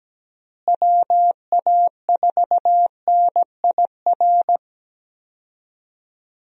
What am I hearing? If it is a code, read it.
Morse code, heard as a keyed tone, about 17 wpm, 700 Hz. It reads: WA4NIR
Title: USA Call Signs - 6 Characters - Repeated 3x - 17wpm